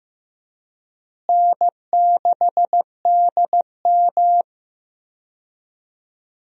Morse code, heard as a keyed tone, about 15 wpm, 700 Hz. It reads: N6DM
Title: USA Call Signs - 4 Characters - Repeated 3x - 15wpm